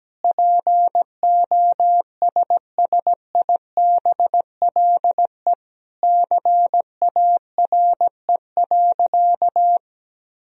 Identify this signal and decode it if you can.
Morse code, heard as a keyed tone, about 17 wpm, 700 Hz. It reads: POSSIBLE CARE.
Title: Sentences from Top 1000 Words - Review Part 2 - 17wpm